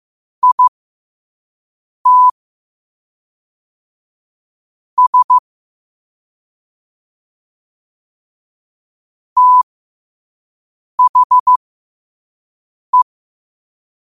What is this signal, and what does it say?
Morse code, heard as a keyed tone, about 15 wpm, spaced as for 5 wpm, 1000 Hz. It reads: IT S THE
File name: qotdfiles / 5wpm